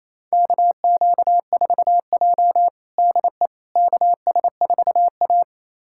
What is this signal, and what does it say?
Morse code, heard as a keyed tone, about 28 wpm, 700 Hz. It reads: KQ4J DE KS4A
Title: USA Callsign DE Callsign - 4 Characters - 28wpm